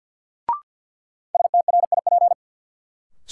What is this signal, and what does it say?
Morse code, heard as a keyed tone, about 47 wpm, 700 Hz. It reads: STRIP
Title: Sentences from Top 1600 Words - New Word Drill - 50wpm